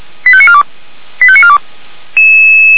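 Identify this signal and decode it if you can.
Five-tone selective call (ZVEI), heard as quick runs of five tones, tones of about 70 ms. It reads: 85942 85942
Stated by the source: ZVEI five-tone call 85751 received by SDR, from the Feuerwehr Orlamünde audio server.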